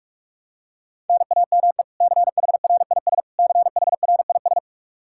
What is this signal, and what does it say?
Morse code, heard as a keyed tone, about 47 wpm, 700 Hz. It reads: NAME CHRIS CHRIS
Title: Practice QSO - Exchange 1 - Beginning - 45wpm